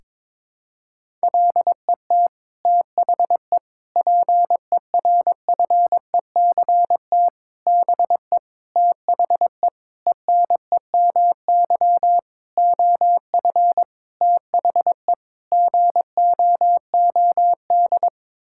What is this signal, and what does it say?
Morse code, heard as a keyed tone, about 22 wpm, 700 Hz. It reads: LET THE PERFECT BE THE ENEMY OF THE GOOD
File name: Sets of 9 Words - 22wpm